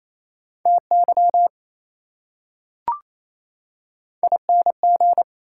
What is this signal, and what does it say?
Morse code, heard as a keyed tone, about 28 wpm, 700 Hz. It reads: TY ING
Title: Prefixes and Suffixes 28wpm